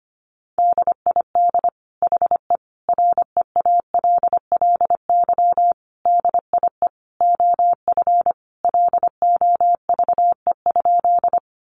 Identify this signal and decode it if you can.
Morse code, heard as a keyed tone, about 25 wpm, 700 Hz. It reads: DID HE REALLY DIE OF LOVE?
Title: Sentences from Top 1300 Words 25wpm